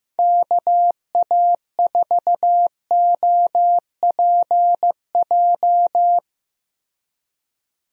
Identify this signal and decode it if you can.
Morse code, heard as a keyed tone, about 15 wpm, 700 Hz. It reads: KA4OPJ
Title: USA Call Signs - 6 Characters - 15wpm